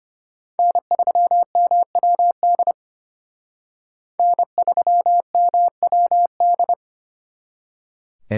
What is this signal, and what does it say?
Morse code, heard as a keyed tone, about 32 wpm, 700 Hz. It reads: N3MWD N3MWD